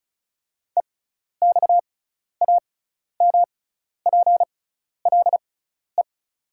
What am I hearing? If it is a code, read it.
Morse code, heard as a keyed tone, about 36 wpm, spaced as for 12 wpm, 690 Hz. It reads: EXAMPLE
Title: 6x Character Spacing - Top 300 Words - 35wpm